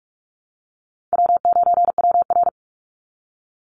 Morse code, heard as a keyed tone, about 46 wpm, 700 Hz. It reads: W9WR
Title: Easy Call Signs - 45wpm